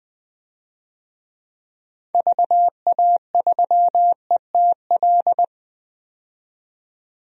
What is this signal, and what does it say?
Morse code, heard as a keyed tone, about 21 wpm, 700 Hz. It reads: VA3ETL
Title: DXCC-100 Call Signs - 6 Characters - 20wpm